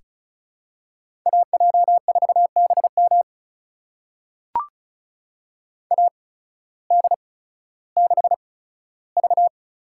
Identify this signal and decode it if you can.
Morse code, heard as a keyed tone, about 36 wpm, spaced as for 16 wpm, 700 Hz. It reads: AJ4BM AD6V
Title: USA Call Signs - 5 Characters - 8x Character Spacing - 35wpm